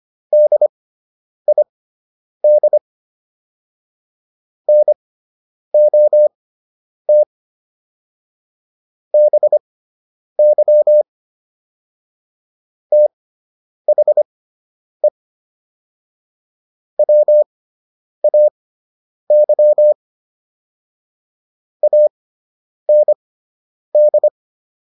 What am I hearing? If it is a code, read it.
Morse code, heard as a keyed tone, about 25 wpm, spaced as for 9 wpm, 600 Hz. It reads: DID NOT BY THE WAY AND